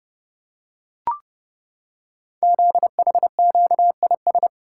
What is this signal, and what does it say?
Morse code, heard as a keyed tone, about 30 wpm, 690 Hz. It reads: ZHQIS
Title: Code Groups - 5 Letters - 30wpm